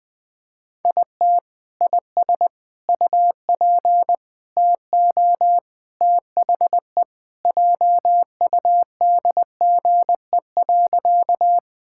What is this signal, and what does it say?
Morse code, heard as a keyed tone, about 21 wpm, 700 Hz. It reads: IT IS UP TO THE JUDGE.
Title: Sentences with 3-Word Phrases - 20wpm